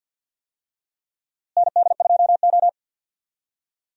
Morse code, heard as a keyed tone, about 52 wpm, 700 Hz. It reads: ND2O
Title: USA Call Signs - 4 Characters - Repeated 2x - 50wpm